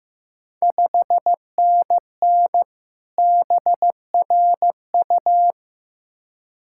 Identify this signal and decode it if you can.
Morse code, heard as a keyed tone, about 15 wpm, 700 Hz. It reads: 5NN BRU
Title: Ontario QSO Party Exchange 15wpm